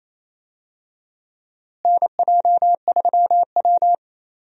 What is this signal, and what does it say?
Morse code, heard as a keyed tone, about 28 wpm, 700 Hz. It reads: NJ3W
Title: USA Call Signs - 4 Characters - Repeated 2x - 28wpm